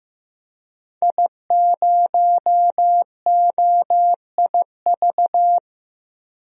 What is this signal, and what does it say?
Morse code, heard as a keyed tone, about 15 wpm, 700 Hz. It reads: I0OIV
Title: DXCC-100 Call Signs - 5 Characters - 15wpm